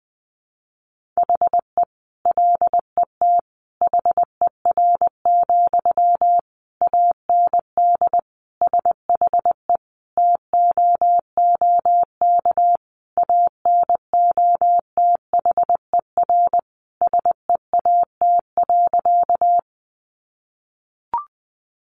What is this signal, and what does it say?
Morse code, heard as a keyed tone, about 20 wpm, 700 Hz. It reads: HE LET HER, AND SHE TOOK ANOTHER SEAT.
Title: Sentences from Top 1500 Words 20wpm